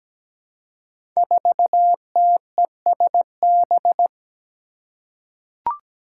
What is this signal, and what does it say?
Morse code, heard as a keyed tone, about 17 wpm, 700 Hz. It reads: 4TESB
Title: Five Challenging Letter-Number Combinations 17wpm